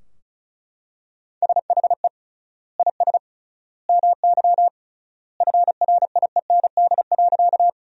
Morse code, heard as a keyed tone, about 34 wpm, 700 Hz. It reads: SHE IS MY FRIEND.
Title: 3x Word-Spacing - Sentences from Top 300 Words - 35wpm